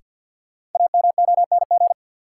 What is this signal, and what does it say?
Morse code, heard as a keyed tone, about 53 wpm, 700 Hz. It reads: AMONG
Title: Sets of 2 Words - New Word Drill - 50wpm